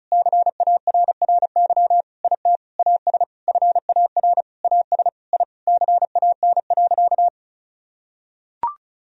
Morse code, heard as a keyed tone, about 36 wpm, 700 Hz. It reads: CARRY IT AS FAR AS I CAN.
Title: Sentences from Top 1000 Words - Review Part 1 - 35wpm